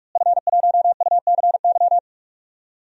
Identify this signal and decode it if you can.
Morse code, heard as a keyed tone, about 41 wpm, 700 Hz. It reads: F1UCY
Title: DXCC-100 Call Signs - 5 Characters - Repeated 3x - 45wpm